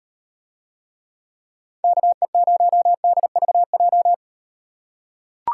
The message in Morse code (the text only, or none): KE0DVJ